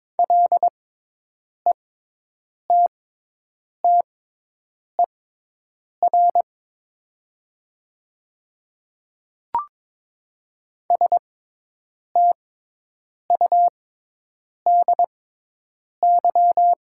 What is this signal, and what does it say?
Morse code, heard as a keyed tone, about 21 wpm, spaced as for 8 wpm, 700 Hz. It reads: LETTER STUDY
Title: 6x Character Spacing - Top 300 Words - 22wpm